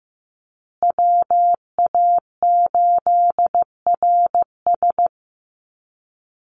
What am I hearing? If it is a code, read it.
Morse code, heard as a keyed tone, about 15 wpm, 700 Hz. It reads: WA8RS
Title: USA Call Signs Repeated 3x 15wpm